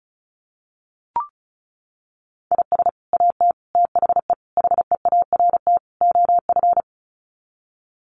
Morse code, heard as a keyed tone, about 34 wpm, 700 Hz. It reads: IS AT THE HEART OF
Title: Sets of 5 Words Encore2 35wpm